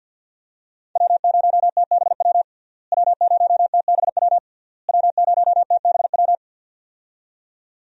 Morse code, heard as a keyed tone, about 50 wpm, 700 Hz. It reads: W0TBW W0TBW W0TBW
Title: USA Call Signs - 5 Characters - Repeated 3x - 50wpm